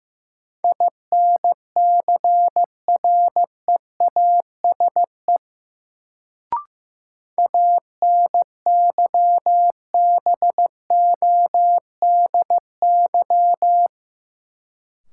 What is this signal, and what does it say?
Morse code, heard as a keyed tone, about 15 wpm, 700 Hz. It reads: INCREASE ANYBODY
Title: Sets of 5 Words - New Word Drill - 15wpm